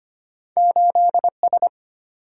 Morse code, heard as a keyed tone, about 25 wpm, 700 Hz. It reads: 8S
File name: Two Letter-Number Combinations - 25wpm